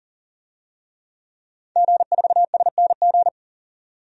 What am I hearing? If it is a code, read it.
Morse code, heard as a keyed tone, about 40 wpm, 700 Hz. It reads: G4SNG
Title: DXCC-100 Call Signs - 5 Characters - 40wpm